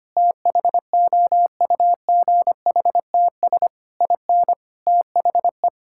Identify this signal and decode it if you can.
Morse code, heard as a keyed tone, about 25 wpm, 700 Hz. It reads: THOUGHTS IN THE
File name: practice / 25wpm / Sentences with 9-Word Phrases - 25wpm